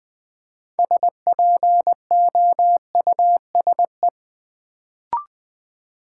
Morse code, heard as a keyed tone, about 20 wpm, 700 Hz. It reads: SPOUSE
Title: Build A Word - 2 Letters At A Time - 6-Letter Words - 20wpm